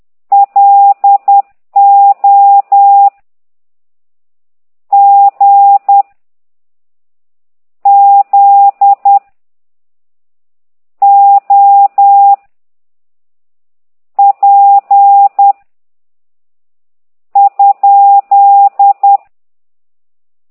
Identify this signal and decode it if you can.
Morse code, heard as a keyed tone, about 10 wpm, 800 Hz. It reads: L O G Z O P ?
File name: ct2idw / exerc15